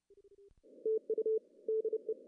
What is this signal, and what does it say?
Morse code, heard as a keyed tone, about 32 wpm, 410 Hz. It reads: TU DE
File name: CW / cqww-cw-k5tr